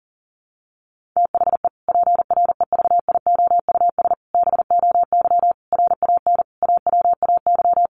Morse code, heard as a keyed tone, about 40 wpm, 700 Hz. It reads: THE PREVIOUS BOY RAN AWAY
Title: Sentences from Top 1500 Words 40wpm